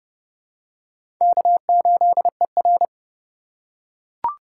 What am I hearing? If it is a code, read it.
Morse code, heard as a keyed tone, about 30 wpm, 700 Hz. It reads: K8ER
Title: USA Call Signs - 4 Characters - 4x Character Spacing - 30wpm